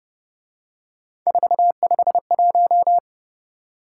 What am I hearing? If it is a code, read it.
Morse code, heard as a keyed tone, about 31 wpm, 700 Hz. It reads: 451